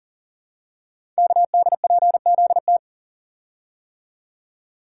Morse code, heard as a keyed tone, about 38 wpm, 700 Hz. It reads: KDPZT
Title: Code Groups - 5 Letters - 40wpm